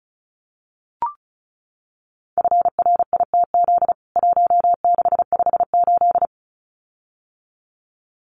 Morse code, heard as a keyed tone, about 34 wpm, 700 Hz. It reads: FRITZ 1658
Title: CWT Exchange 35wpm